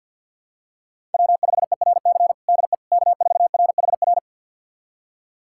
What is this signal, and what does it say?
Morse code, heard as a keyed tone, about 52 wpm, 700 Hz. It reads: W5ERC DE K4RHR
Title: USA Callsign DE Callsign 50wpm